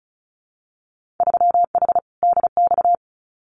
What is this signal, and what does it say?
Morse code, heard as a keyed tone, about 36 wpm, 700 Hz. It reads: 3H DX
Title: Winter Field Day Exchanges 35wpm